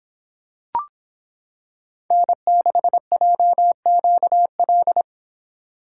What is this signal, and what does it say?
Morse code, heard as a keyed tone, about 25 wpm, 700 Hz. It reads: N6JQL